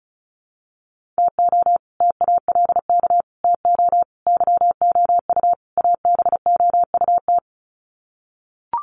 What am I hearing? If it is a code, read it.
Morse code, heard as a keyed tone, about 34 wpm, 700 Hz. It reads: TO TALK TO YOU ABOUT
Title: Sets of 5 Words Encore 35wpm